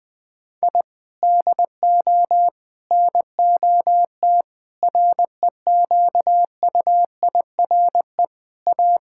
Morse code, heard as a keyed tone, about 19 wpm, 700 Hz. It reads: I DO NOT REQUIRE A
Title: Sentences from Top 1000 Words - Review Part 6 - 20wpm